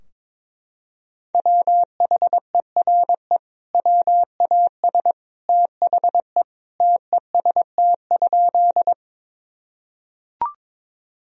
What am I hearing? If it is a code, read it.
Morse code, heard as a keyed tone, about 22 wpm, 700 Hz. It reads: WHERE WAS THE TEST?
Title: Sentences from Top 1000 Words - Review Part 3 - 22wpm